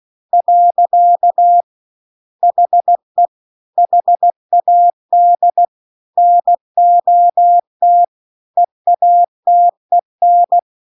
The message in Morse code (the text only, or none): . HE HAD NOT EATEN